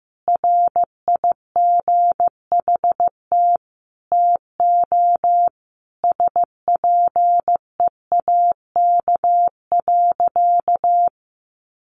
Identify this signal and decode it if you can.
Morse code, heard as a keyed tone, about 15 wpm, 700 Hz. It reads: RIGHT TO SPEAK.
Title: Sentences from Top 1300 Words 15wpm